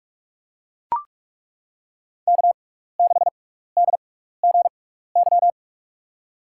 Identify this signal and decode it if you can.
Morse code, heard as a keyed tone, about 42 wpm, spaced as for 15 wpm, 700 Hz. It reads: K6DGY
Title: USA Call Signs - 5 Characters - 6x Character Spacing - 45wpm